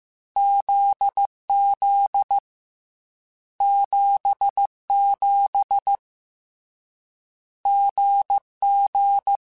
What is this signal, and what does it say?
Morse code, heard as a keyed tone, about 15 wpm, 800 Hz. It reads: ZZ 77 GG
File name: Short Runs / Short Runs-mp3s / GZ GZ7 @13